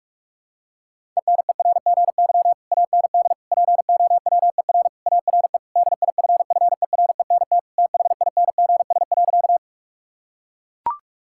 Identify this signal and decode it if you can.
Morse code, heard as a keyed tone, about 41 wpm, 700 Hz. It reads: ENERGY AND POWER ARE DIFFERENT THINGS.
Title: Sentences from Top 1000 Words - Review Part 6 - 45wpm